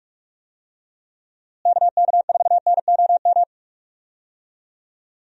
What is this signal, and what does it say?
Morse code, heard as a keyed tone, about 42 wpm, 700 Hz. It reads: KK4NOM
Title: USA Call Signs - 6 Characters - Repeated 3x - 45wpm